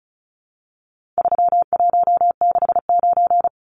Morse code, heard as a keyed tone, about 34 wpm, 700 Hz. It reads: 3169
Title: Numbers - 4-Digit 35wpm